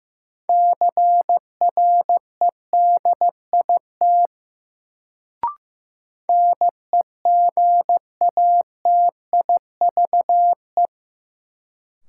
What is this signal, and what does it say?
Morse code, heard as a keyed tone, about 15 wpm, 700 Hz. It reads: CREDIT NEGATIVE
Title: Sentences from Top 1000 Words - New Word Drill - 15wpm